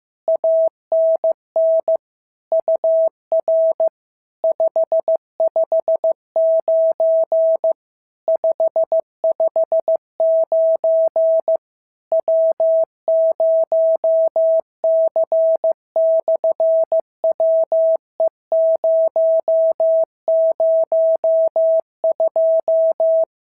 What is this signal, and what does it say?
Morse code, heard as a keyed tone, about 15 wpm, 650 Hz. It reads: ANN UR 559 559 W0C/WE002
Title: SOTA QSO Practice 15wpm